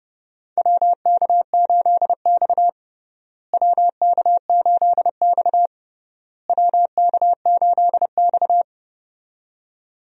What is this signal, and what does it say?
Morse code, heard as a keyed tone, about 30 wpm, 700 Hz. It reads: WK8X WK8X WK8X
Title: USA Call Signs - 4 Characters - Repeated 3x - 30wpm